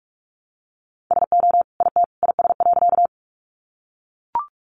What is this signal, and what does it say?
Morse code, heard as a keyed tone, about 44 wpm, 700 Hz. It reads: SO IT IS.
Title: Sentences from Top 100 Words 45wpm